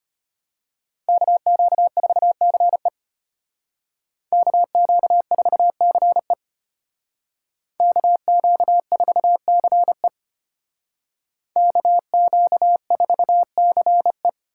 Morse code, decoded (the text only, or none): KQ4CE KQ4CE KQ4CE KQ4CE